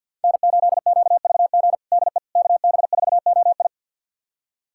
Morse code, heard as a keyed tone, about 51 wpm, 700 Hz. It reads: N8QVG DE KB4OI
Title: USA Callsign DE Callsign - 5 Characters - 50wpm